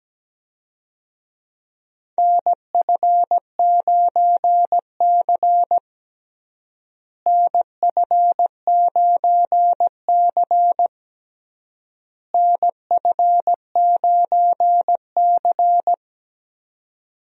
Morse code, heard as a keyed tone, about 17 wpm, 700 Hz. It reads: NF9C NF9C NF9C
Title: USA Call Signs - 4 Characters - Repeated 3x - 17wpm